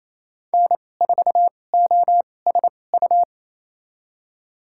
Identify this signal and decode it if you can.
Morse code, heard as a keyed tone, about 28 wpm, spaced as for 20 wpm, 700 Hz. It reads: N4OSU
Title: USA Call Signs - 5 Characters - 2x Character Spacing - 28wpm